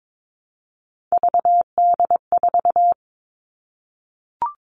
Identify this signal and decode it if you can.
Morse code, heard as a keyed tone, about 22 wpm, 700 Hz. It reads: VD4
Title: Three Challenging Letter-Number Combinations 22wpm